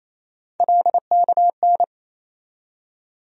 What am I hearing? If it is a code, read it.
Morse code, heard as a keyed tone, about 28 wpm, 700 Hz. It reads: LKN
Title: USA Call Sign Postfixes - 3 Letters 28wpm